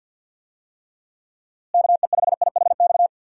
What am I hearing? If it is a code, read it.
Morse code, heard as a keyed tone, about 48 wpm, 700 Hz. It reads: KE5IHX
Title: USA Call Signs - 6 Characters - Repeated 3x - 50wpm